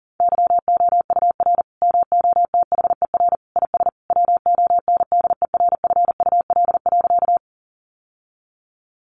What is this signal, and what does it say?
Morse code, heard as a keyed tone, about 40 wpm, 700 Hz. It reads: YOUR MOTHER IS WONDERFUL.
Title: Sentences from Top 1100 Words 40wpm